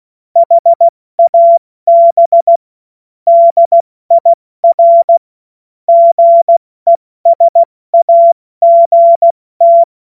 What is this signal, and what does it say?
Morse code, heard as a keyed tone, about 16 wpm, 680 Hz. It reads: HAB DIR GESAGT
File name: Morsekurs / Witze / Witz 26. Oktober 2021, 70 F80